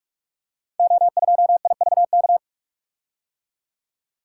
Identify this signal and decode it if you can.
Morse code, heard as a keyed tone, about 44 wpm, 700 Hz. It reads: O2IVK
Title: Code Groups - 5 Letter:Numbers - 45wpm